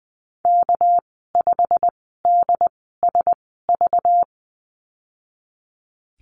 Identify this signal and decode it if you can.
Morse code, heard as a keyed tone, about 20 wpm, spaced as for 14 wpm, 700 Hz. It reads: K5DSV
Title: 219 Lesson 35g - Number 0 - Call Signs 20wpm